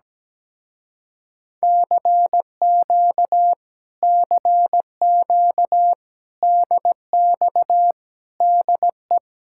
Morse code, heard as a keyed tone, about 17 wpm, 700 Hz. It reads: CQ CQ DX DE
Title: Speed-Racing Common QSO Phrases - 17wpm